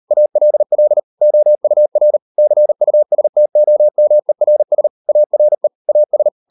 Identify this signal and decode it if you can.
Morse code, heard as a keyed tone, about 38 wpm, 600 Hz. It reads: ALL OUR CUSTOMERS ARE AS